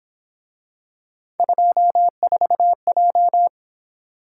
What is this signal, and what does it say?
Morse code, heard as a keyed tone, about 27 wpm, 700 Hz. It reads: 24J
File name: Speed-Racing - Three Letter-Number Combinations - 22wpm